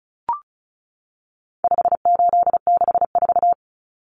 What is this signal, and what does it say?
Morse code, heard as a keyed tone, about 34 wpm, 700 Hz. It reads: 5864
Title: Numbers - 4-Digit 35wpm